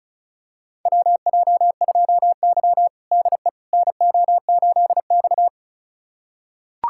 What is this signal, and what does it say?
Morse code, heard as a keyed tone, about 36 wpm, 700 Hz. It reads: WJ2Y DE NO8X
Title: USA Callsign DE Callsign - 4 Characters - 35wpm